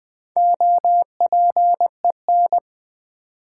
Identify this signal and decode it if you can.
Morse code, heard as a keyed tone, about 20 wpm, 700 Hz. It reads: OPEN
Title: Sentences from Top 300 Words - New Word Drill - 20wpm